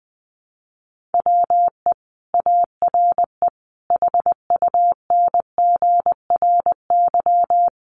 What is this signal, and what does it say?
Morse code, heard as a keyed tone, about 20 wpm, 700 Hz. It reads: WE ARE HUNGRY
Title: Sentences from Top 1400 Words 20wpm